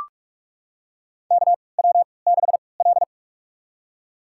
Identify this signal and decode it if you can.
Morse code, heard as a keyed tone, about 43 wpm, spaced as for 25 wpm, 700 Hz. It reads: KW6L
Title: USA Call Signs - 4 Characters - 3x Character Spacing - 45wpm